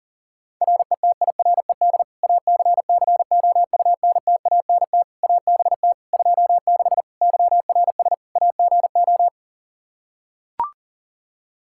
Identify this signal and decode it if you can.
Morse code, heard as a keyed tone, about 37 wpm, 700 Hz. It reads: RETIRED ACCOUNTANT ABT 26 YRS AGO
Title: Practice QSO - Exchange 3 - More Info - 40wpm